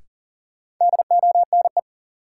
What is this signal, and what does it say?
Morse code, heard as a keyed tone, about 40 wpm, 700 Hz. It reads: DONE
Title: Sets of 5 Words Encore - New Word Drill - 40wpm